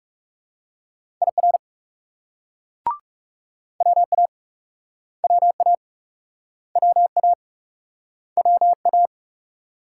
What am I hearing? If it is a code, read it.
Morse code, heard as a keyed tone, about 41 wpm, 700 Hz. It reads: IR WA WA WA WA